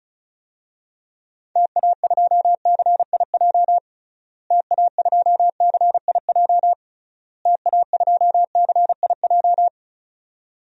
Morse code, heard as a keyed tone, about 36 wpm, 700 Hz. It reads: TA2CIJ TA2CIJ TA2CIJ
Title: DXCC-100 Call Signs - 6 Characters - Repeated 3x - 35wpm